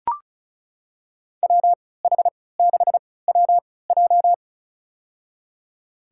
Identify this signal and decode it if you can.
Morse code, heard as a keyed tone, about 34 wpm, spaced as for 20 wpm, 700 Hz. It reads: WH6WJ
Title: USA Call Signs - 5 Characters - 3x Character Spacing - 35wpm